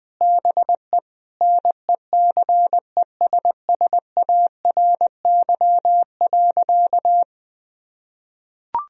Morse code, heard as a keyed tone, about 20 wpm, 700 Hz. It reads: BE NECESSARY.
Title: Sentences from Top 1000 Words - Review Part 3 - 20wpm